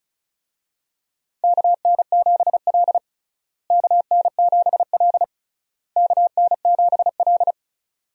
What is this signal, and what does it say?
Morse code, heard as a keyed tone, about 36 wpm, 700 Hz. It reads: KN7L KN7L KN7L
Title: USA Call Signs - 4 Characters - Repeated 3x - 35wpm